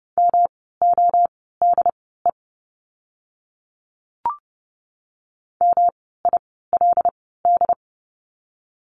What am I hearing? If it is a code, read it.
Morse code, heard as a keyed tone, about 30 wpm, spaced as for 17 wpm, 700 Hz. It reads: MODE MILD
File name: course / 30:10wpm / 108 Lesson 18f - Letter m - Words 30wpm